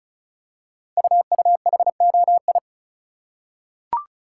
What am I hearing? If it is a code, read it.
Morse code, heard as a keyed tone, about 36 wpm, 700 Hz. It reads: UUHOI
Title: Code Groups - 5 Letters - 35wpm